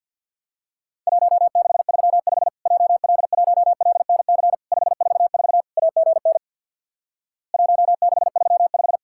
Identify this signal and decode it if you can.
Morse code, heard as a keyed tone, about 52 wpm, 700 Hz. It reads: 1635 JL1LNC 544 AGN 1635
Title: Worked All Europe DX Contest - QTC Exchange with Repeats - 50wpm